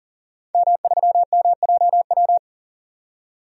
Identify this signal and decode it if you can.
Morse code, heard as a keyed tone, about 40 wpm, 700 Hz. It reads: M3MJW
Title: DXCC-100 Call Signs - 5 Characters - 40wpm